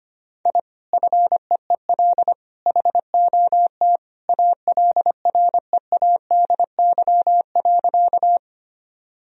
Morse code, heard as a keyed tone, about 25 wpm, 700 Hz. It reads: I FEEL HOT ALREADY.
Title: Sentences from Top 1000 Words - Review Part 3 - 25wpm